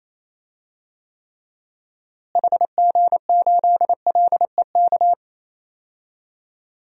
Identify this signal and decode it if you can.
Morse code, heard as a keyed tone, about 28 wpm, 700 Hz. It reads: HG8LEK
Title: DXCC-100 Call Signs - 6 Characters - Repeated 2x - 28wpm